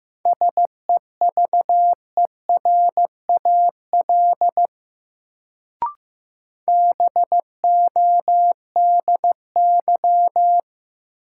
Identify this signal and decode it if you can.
Morse code, heard as a keyed tone, about 15 wpm, 700 Hz. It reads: SEVERAL BODY
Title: Sentences from Top 400 Words - New Word Drill - 15wpm